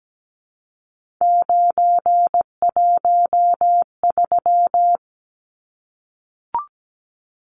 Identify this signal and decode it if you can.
Morse code, heard as a keyed tone, about 17 wpm, 700 Hz. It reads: 913
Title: Numbers - 3-Digit 17wpm